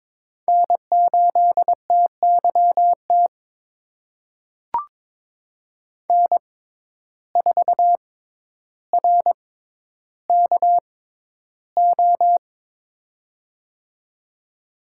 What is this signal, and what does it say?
Morse code, heard as a keyed tone, about 22 wpm, spaced as for 11 wpm, 700 Hz. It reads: N8TYT N4RKO
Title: USA Call Signs - 5 Characters - 6x Character Spacing - 22wpm